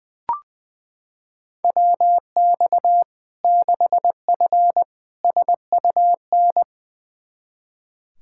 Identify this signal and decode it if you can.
Morse code, heard as a keyed tone, about 19 wpm, 700 Hz. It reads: WX 6F SUN
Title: Practice QSO - Exchange 2 - Info - 20wpm